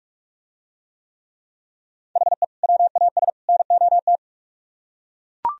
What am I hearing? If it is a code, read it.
Morse code, heard as a keyed tone, about 42 wpm, 700 Hz. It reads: HE WAS NOT